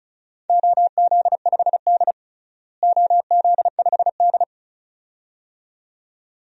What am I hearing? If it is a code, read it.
Morse code, heard as a keyed tone, about 36 wpm, 700 Hz. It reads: OZ5D OZ5D
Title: DXCC-100 Call Signs - 4 Characters - Repeated 2x - 35wpm